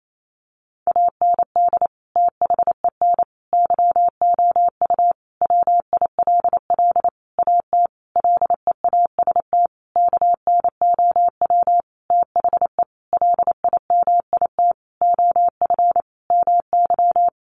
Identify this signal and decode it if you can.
Morse code, heard as a keyed tone, about 28 wpm, 700 Hz. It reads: AND THEN YOU WILL AT LEAST KNOW THE LIMIT OF MY
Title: Sentences from Top 1500 Words 28wpm